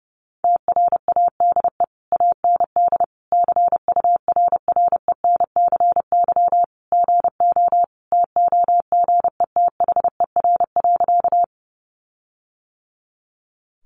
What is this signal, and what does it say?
Morse code, heard as a keyed tone, about 30 wpm, 700 Hz. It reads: TRADE AND CURRENCY GO TOGETHER.
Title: Sentences from Top 1500 Words 30wpm